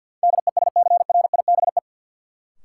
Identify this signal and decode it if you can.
Morse code, heard as a keyed tone, about 47 wpm, 700 Hz. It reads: DESCRIBE
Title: Sentences from Top 1000 Words - New Word Drill - 50wpm